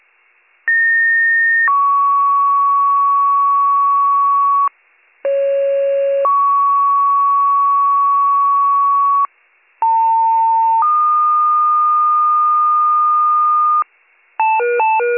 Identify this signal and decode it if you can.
Two-tone page, heard as a one-second tone, then a three-second tone, about 1850 then 1120 Hz, about 550 then 1120 Hz, about 900 then 1220 Hz. It is heard from 0.7 to 13.8 s.